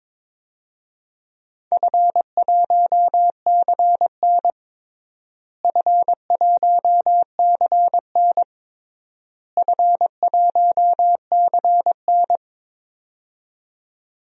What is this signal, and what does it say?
Morse code, heard as a keyed tone, about 22 wpm, 700 Hz. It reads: F1CN F1CN F1CN
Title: DXCC-100 Call Signs - 4 Characters - Repeated 3x - 22wpm